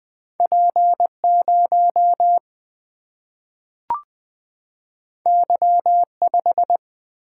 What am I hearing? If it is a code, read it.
Morse code, heard as a keyed tone, about 20 wpm, 700 Hz. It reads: P 0 Y 5
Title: Single-Letter Single-Number 20wpm